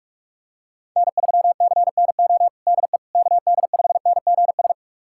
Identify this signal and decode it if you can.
Morse code, heard as a keyed tone, about 46 wpm, 690 Hz. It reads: N3CNO DE KD5NGS
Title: USA Callsign DE Callsign 45wpm